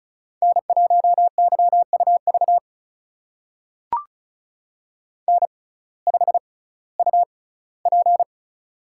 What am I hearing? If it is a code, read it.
Morse code, heard as a keyed tone, about 36 wpm, spaced as for 19 wpm, 700 Hz. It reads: N1YUV N5UP
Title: USA Call Signs - 6x Character Spacing - 35wpm